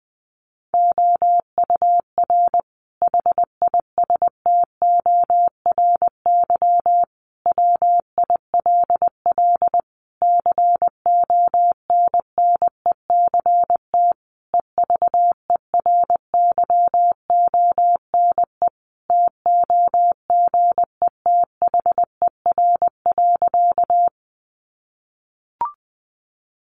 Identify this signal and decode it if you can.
Morse code, heard as a keyed tone, about 20 wpm, 700 Hz. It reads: OUR HISTORY WILL CONNECT EVERYONE TOGETHER.
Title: Sentences from Top 1500 Words 20wpm